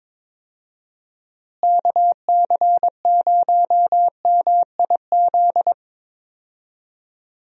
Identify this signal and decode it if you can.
Morse code, heard as a keyed tone, about 21 wpm, 700 Hz. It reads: KC0MIZ
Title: USA Call Signs - 6 Characters - Repeated 2x - 22wpm